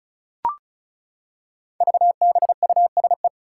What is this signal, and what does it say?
Morse code, heard as a keyed tone, about 36 wpm, 700 Hz. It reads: VBUSE
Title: Five Challenging Letter-Number Combinations 35wpm